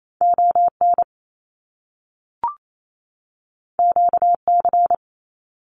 Morse code, heard as a keyed tone, about 28 wpm, 700 Hz. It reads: ON QC
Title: Canadian Provinces and Territories 28wpm